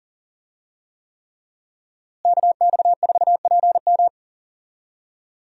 Morse code, heard as a keyed tone, about 40 wpm, 700 Hz. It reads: KX4PM
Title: USA Call Signs - 5 Characters - Repeated 3x - 40wpm